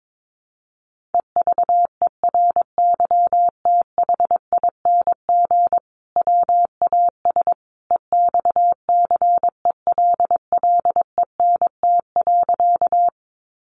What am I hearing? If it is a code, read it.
Morse code, heard as a keyed tone, about 22 wpm, 700 Hz. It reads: EVERYTHING WAS EXCELLENT.
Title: Sentences from Top 1000 Words 22wpm